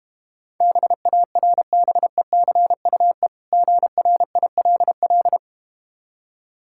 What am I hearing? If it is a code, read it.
Morse code, heard as a keyed tone, about 32 wpm, 700 Hz. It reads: BARBECUE GRILL